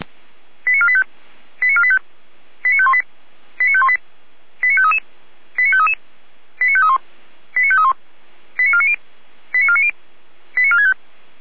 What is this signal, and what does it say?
Five-tone selective call (ZVEI), heard as quick runs of five tones, tones of about 70 ms. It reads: 89485 89485 89518 89518 89533 89533 89531 89531 89490 89490 89465